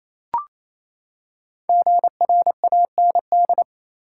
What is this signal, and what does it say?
Morse code, heard as a keyed tone, about 28 wpm, 700 Hz. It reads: GRAND
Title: Five Letter Words - 28wpm